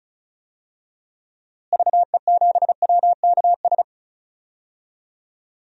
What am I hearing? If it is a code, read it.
Morse code, heard as a keyed tone, about 36 wpm, 700 Hz. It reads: VE7WKS
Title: DXCC-100 Call Signs - 6 Characters - Repeated 3x - 35wpm